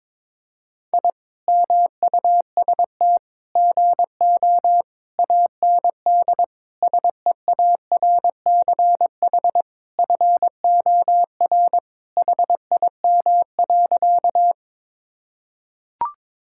Morse code, heard as a keyed tone, about 21 wpm, 700 Hz. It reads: I MUST GO AND SEARCH FOR HIM.
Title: Sentences from Top 1000 Words - Review Part 5 - 22wpm